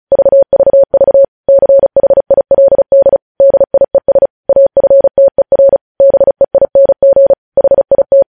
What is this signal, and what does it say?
Morse code, heard as a keyed tone, about 36 wpm, 550 Hz. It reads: VVV CHILD DIES AFTER BEING HIT
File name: podcast / News headlines-1x-35x35